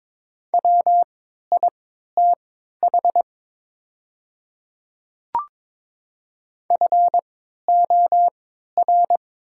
Morse code, heard as a keyed tone, about 22 wpm, spaced as for 12 wpm, 690 Hz. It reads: WITH FOR
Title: 3x Character Spacing - Top 100 Words - 22wpm